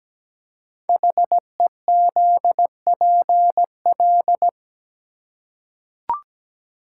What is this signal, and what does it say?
Morse code, heard as a keyed tone, about 17 wpm, 700 Hz. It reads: HEZPL
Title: Code Groups - 5 Letters - 17wpm